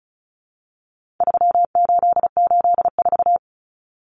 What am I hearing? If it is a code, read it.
Morse code, heard as a keyed tone, about 34 wpm, 700 Hz. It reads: 3884
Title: Common Cut Numbers - 4-Digit 35wpm